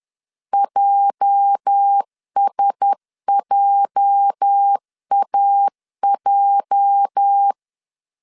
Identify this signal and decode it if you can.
Morse code, heard as a keyed tone, about 10 wpm, 800 Hz. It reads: JSJAJ
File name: LEZIONI / 8 YJ45 / 8 YJ 10wpm 2651